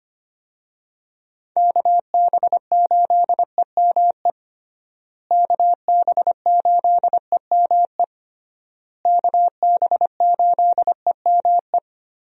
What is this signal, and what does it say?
Morse code, heard as a keyed tone, about 25 wpm, 700 Hz. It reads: KB8EME KB8EME KB8EME
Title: USA Call Signs - 6 Characters - Repeated 3x - 25wpm